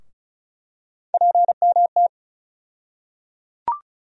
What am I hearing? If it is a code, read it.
Morse code, heard as a keyed tone, about 34 wpm, 700 Hz. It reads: PMT
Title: USA Call Sign Postfixes - 3 Letters 35wpm